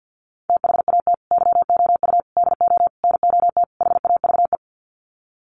Morse code, heard as a keyed tone, about 50 wpm, 700 Hz. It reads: THAT YOU DO NOT HAVE
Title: Sets of 5 Words Encore 50wpm